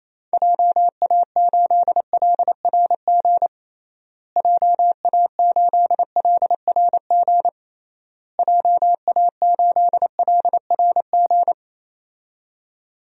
Morse code, heard as a keyed tone, about 27 wpm, 700 Hz. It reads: JA8LRG JA8LRG JA8LRG
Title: DXCC-100 Call Signs - 6 Characters - Repeated 3x - 28wpm